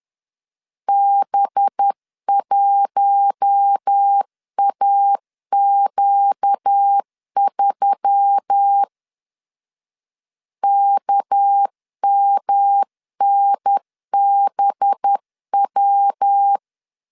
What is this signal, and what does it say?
Morse code, heard as a keyed tone, about 10 wpm, 800 Hz. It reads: B1AQ3 KMNBW
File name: LEZIONI / 7 GF9Z / 7 G 10wpm 2501